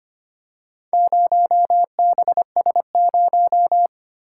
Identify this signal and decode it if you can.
Morse code, heard as a keyed tone, about 25 wpm, 700 Hz. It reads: 0BS0
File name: practice / 25wpm / Four Letter-Number Combinations - 25wpm